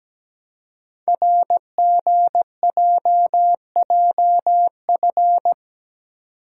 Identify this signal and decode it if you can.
Morse code, heard as a keyed tone, about 17 wpm, 700 Hz. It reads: RGJJF
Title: Code Groups - 5 Letters - 17wpm